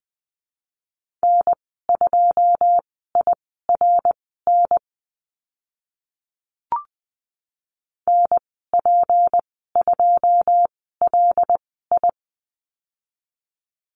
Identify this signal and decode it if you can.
Morse code, heard as a keyed tone, about 20 wpm, spaced as for 14 wpm, 700 Hz. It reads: N2IRN NP2LI
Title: 161 Lesson 26g - Letter g - Call Signs 20wpm